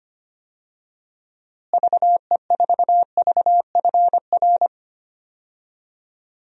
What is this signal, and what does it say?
Morse code, heard as a keyed tone, about 25 wpm, 700 Hz. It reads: VE4VFR
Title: DXCC-100 Call Signs - 6 Characters - Repeated 2x - 25wpm